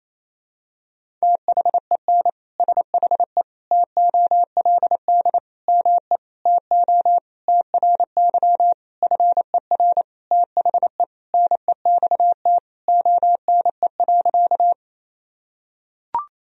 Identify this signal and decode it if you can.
Morse code, heard as a keyed tone, about 27 wpm, 700 Hz. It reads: THEN SHE TOLD ME TO TRY FER THE NEXT ONE.
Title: Sentences from Top 400 Words and Abbreviations 28wpm